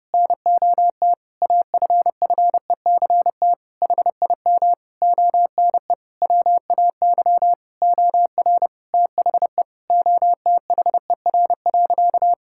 Morse code, heard as a keyed tone, about 30 wpm, 700 Hz. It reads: NOT AFFECT HIM ONE WAY OR THE OTHER.
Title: Sentences from Top 1000 Words - Review Part 6 - 30wpm